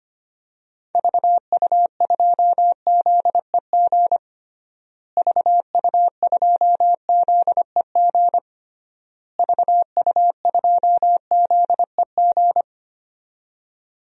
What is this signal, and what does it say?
Morse code, heard as a keyed tone, about 25 wpm, 700 Hz. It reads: VU2ZEG VU2ZEG VU2ZEG
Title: DXCC-100 Call Signs - 6 Characters - Repeated 3x - 25wpm